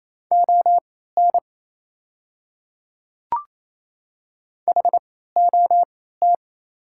Morse code, heard as a keyed tone, about 28 wpm, spaced as for 16 wpm, 700 Hz. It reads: ON HOT
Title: 3x Character Spacing - Top 100 Words - 28wpm